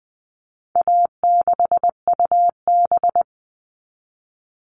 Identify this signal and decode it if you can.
Morse code, heard as a keyed tone, about 20 wpm, 700 Hz. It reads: A6UB
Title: All Cut Numbers - 4-Digit 20wpm